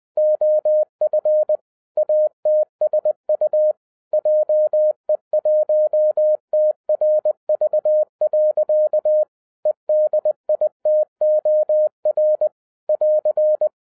Morse code, heard as a keyed tone, about 20 wpm, 610 Hz. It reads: OF ATSU JE1TRV. EDITOR <AR>
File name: 2011-07-mp3 / n48 JOHN K3TN 20wpm